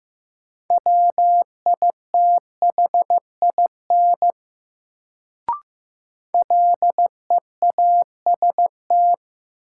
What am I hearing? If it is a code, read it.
Morse code, heard as a keyed tone, about 15 wpm, 700 Hz. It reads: WITHIN LEAST
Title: Sentences from Top 700 Words - New Word Drill - 15wpm